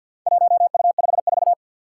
Morse code, heard as a keyed tone, about 48 wpm, 700 Hz. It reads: 1U54
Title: Four Challenging Letter-Number Combinations 50wpm